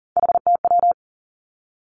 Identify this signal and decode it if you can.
Morse code, heard as a keyed tone, about 38 wpm, 700 Hz. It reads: HTW
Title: Three Challenging Letter-Number Combinations 40wpm